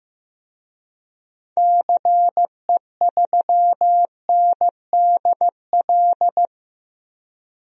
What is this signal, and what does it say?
Morse code, heard as a keyed tone, about 15 wpm, 700 Hz. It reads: CE3NDL
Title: DXCC-100 Call Signs - 6 Characters - 15wpm